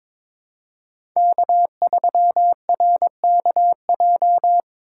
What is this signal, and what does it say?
Morse code, heard as a keyed tone, about 21 wpm, 700 Hz. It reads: K3RKJ
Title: USA Call Signs - 5 Characters - 8x Character Spacing - 22wpm